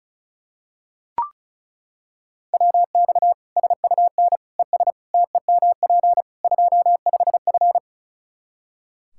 Morse code, heard as a keyed tone, about 36 wpm, 700 Hz. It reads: WX SUN ES TEMP 25F
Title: Practice QSO - Exchange 2 - Info - 35wpm